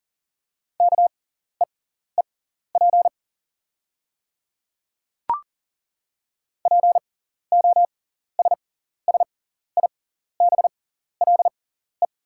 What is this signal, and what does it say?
Morse code, heard as a keyed tone, about 38 wpm, spaced as for 14 wpm, 700 Hz. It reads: KEEP POSSIBLE
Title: 6x Character Spacing - Top 500 Words - 40wpm